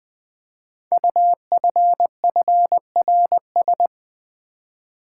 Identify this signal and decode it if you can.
Morse code, heard as a keyed tone, about 20 wpm, 700 Hz. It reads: UFFRS
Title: Code Groups - 5 Letters - 20wpm